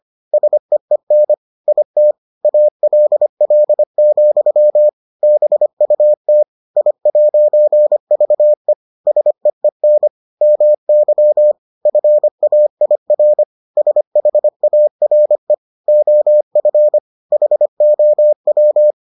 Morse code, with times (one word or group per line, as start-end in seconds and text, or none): 0.3-1.3 SEEN
1.7-2.1 IT
2.4-4.9 ALL,
5.2-6.4 BUT
6.8-8.7 I'VE
9.1-10.1 SEEN
10.4-11.5 MY
11.9-13.4 FAIR
13.8-15.5 SHARE
15.9-17.0 OF
17.3-18.9 HOW